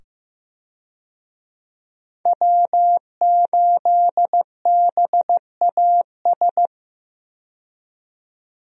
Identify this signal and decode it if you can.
Morse code, heard as a keyed tone, about 15 wpm, 700 Hz. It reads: W8BAS
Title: USA Call Signs - 5 Characters - 15wpm